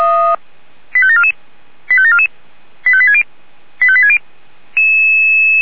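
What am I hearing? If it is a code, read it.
Five-tone selective call (ZVEI), heard as quick runs of five tones, tones of about 70 ms. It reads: 85744 85744 85860 85860
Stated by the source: ZVEI five-tone selective call received on an SDR.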